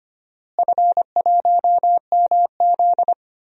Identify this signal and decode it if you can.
Morse code, heard as a keyed tone, about 25 wpm, 700 Hz. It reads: F1MZ
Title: DXCC-100 Call Signs - 4 Characters - Repeated 3x - 25wpm